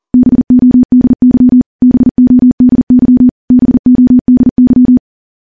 Morse code, heard as a keyed tone, about 40 wpm, 250 Hz. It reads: BODY BODY BODY
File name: morse-training-40WPM